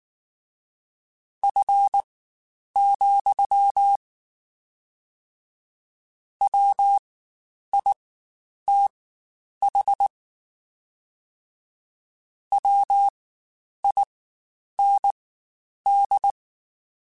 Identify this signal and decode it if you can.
Morse code, heard as a keyed tone, about 19 wpm, spaced as for 9 wpm, 800 Hz. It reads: F, WITH WIND
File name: Spaced Characters / Spaced Characters-mp3s / Weather 4 14 sp @18